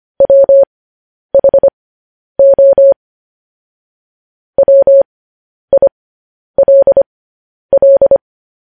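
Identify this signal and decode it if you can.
Morse code, heard as a keyed tone, about 25 wpm, spaced as for 10 wpm, 550 Hz. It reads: WHO WILL
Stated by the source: Morse code news headlines broadcast.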